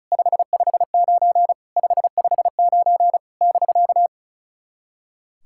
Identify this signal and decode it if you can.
Morse code, heard as a keyed tone, about 36 wpm, 690 Hz. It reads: 559 559 <BK>
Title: SOTA QSO Practice 35wpm